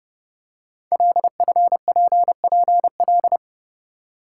Morse code, heard as a keyed tone, about 30 wpm, 700 Hz. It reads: LFPPL
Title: Code Groups - 5 Letters - 30wpm